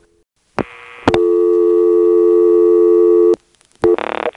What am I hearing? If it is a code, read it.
Dial tone (North America), 350 + 440 Hz.